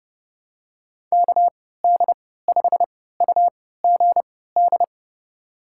Morse code, heard as a keyed tone, about 30 wpm, spaced as for 17 wpm, 700 Hz. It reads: KD5UGD
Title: USA Call Signs - 6 Characters - 3x Character Spacing - 30wpm